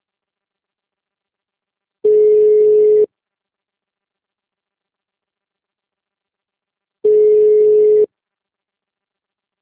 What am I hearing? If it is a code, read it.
Ringback tone (Europe, 425 Hz).